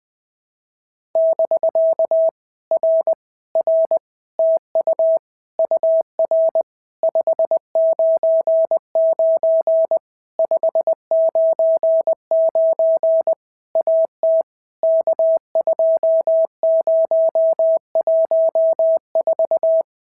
Morse code, heard as a keyed tone, about 20 wpm, 650 Hz. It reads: <BK> R R TU UR 599 599 AT K2014
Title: POTA QSO Practice - 20wpm